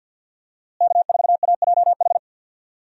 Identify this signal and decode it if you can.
Morse code, heard as a keyed tone, about 52 wpm, 700 Hz. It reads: K4AJH
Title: USA Call Signs - 5 Characters - Repeated 2x - 50wpm